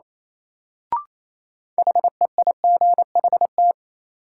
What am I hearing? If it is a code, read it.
Morse code, heard as a keyed tone, about 28 wpm, 700 Hz. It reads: HEIGHT
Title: Sentences from Top 1600 Words - New Word Drill Alt - 28wpm